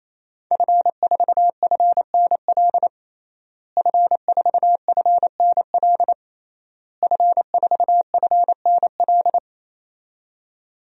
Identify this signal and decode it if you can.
Morse code, heard as a keyed tone, about 27 wpm, 700 Hz. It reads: F4FNL F4FNL F4FNL
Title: DXCC-100 Call Signs - 5 Characters - Repeated 3x - 28wpm